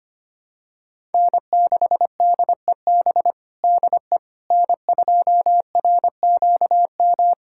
Morse code, heard as a keyed tone, about 25 wpm, 700 Hz. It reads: N6DEB DE N2RQM
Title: USA Callsign DE Callsign - 5 Characters - 25wpm